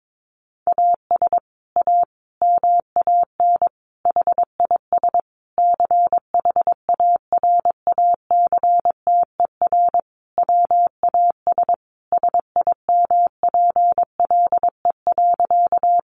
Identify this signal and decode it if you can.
Morse code, heard as a keyed tone, about 22 wpm, 700 Hz. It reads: AS A MAN HIS CHARACTER WAS SIMPLE.
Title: Sentences from Top 1200 Words 22wpm